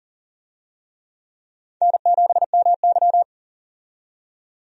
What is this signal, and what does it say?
Morse code, heard as a keyed tone, about 39 wpm, 690 Hz. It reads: N7MY
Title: USA Call Signs - 4 Characters - 40wpm